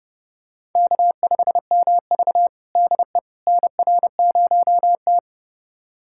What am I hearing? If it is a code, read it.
Morse code, heard as a keyed tone, about 30 wpm, 700 Hz. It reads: K5MV DE NR0T
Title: USA Callsign DE Callsign - 4 Characters - 30wpm